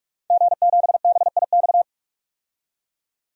Morse code, heard as a keyed tone, about 44 wpm, 690 Hz. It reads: G7BIX
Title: DXCC-100 Call Signs - 5 Characters - 45wpm